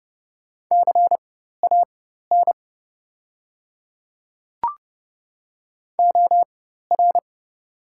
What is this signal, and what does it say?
Morse code, heard as a keyed tone, about 30 wpm, spaced as for 14 wpm, 700 Hz. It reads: CAN OR
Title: 4x Character Spacing - Top 200 Words - 30wpm